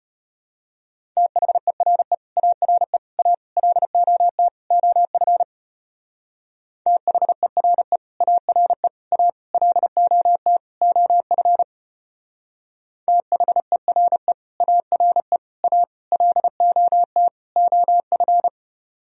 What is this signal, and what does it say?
Morse code, heard as a keyed tone, about 32 wpm, 700 Hz. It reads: THERE ARE A LOT OF THERE ARE A LOT OF THERE ARE A LOT OF